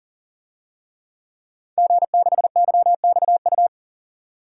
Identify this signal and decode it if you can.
Morse code, heard as a keyed tone, about 40 wpm, 700 Hz. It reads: G6YXU
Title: DXCC-100 Call Signs - 5 Characters - 40wpm